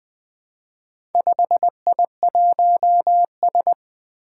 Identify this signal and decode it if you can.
Morse code, heard as a keyed tone, about 20 wpm, 700 Hz. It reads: 5I1S
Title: Four Challenging Letter-Number Combinations 20wpm